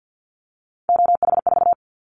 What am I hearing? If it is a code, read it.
Morse code, heard as a keyed tone, about 50 wpm, 700 Hz. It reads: OH4
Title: Callsign Prefixes for Top 25 DXCC - Repeated 3x - 50wpm